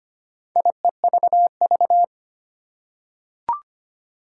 Morse code, heard as a keyed tone, about 25 wpm, 700 Hz. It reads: IEVV
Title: Four Challenging Letter-Number Combinations 25wpm